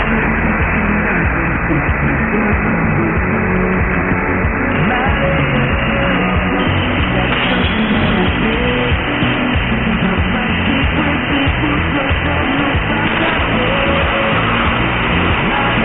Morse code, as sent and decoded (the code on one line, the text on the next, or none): none